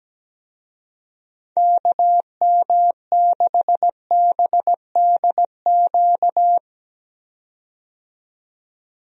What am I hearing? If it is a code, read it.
Morse code, heard as a keyed tone, about 17 wpm, 700 Hz. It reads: KM6BDQ
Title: USA Call Signs - 6 Characters - 17wpm